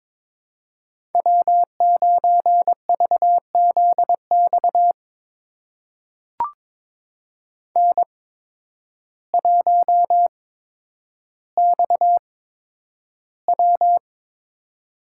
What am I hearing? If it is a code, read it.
Morse code, heard as a keyed tone, about 22 wpm, spaced as for 10 wpm, 700 Hz. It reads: W9VZX N1XW
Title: USA Call Signs - 8x Character Spacing - 22wpm